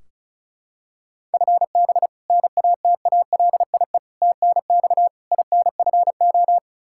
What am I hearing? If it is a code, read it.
Morse code, heard as a keyed tone, about 36 wpm, 700 Hz. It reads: FB NATALIE TNX INFO
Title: Practice QSO - Exchange 3 - More Info - 35wpm